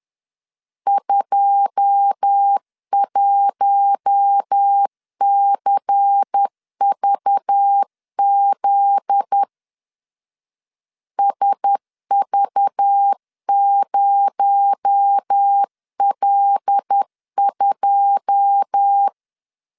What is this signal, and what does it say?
Morse code, heard as a keyed tone, about 10 wpm, 800 Hz. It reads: 21CVZ SV0L2
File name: LEZIONI / 9 2 10wpm 2736